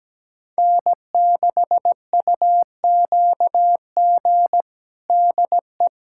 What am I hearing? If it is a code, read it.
Morse code, heard as a keyed tone, about 17 wpm, 700 Hz. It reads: N6UQG DE
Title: USA Callsign DE Callsign - 5 Characters - 17wpm